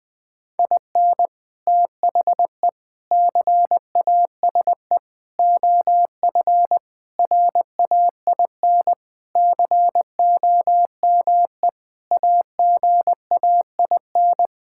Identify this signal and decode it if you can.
Morse code, heard as a keyed tone, about 20 wpm, 700 Hz. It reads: IN THE CASE OF RAIN COME AGAIN
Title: Sentences with 4-Word Phrases - 20wpm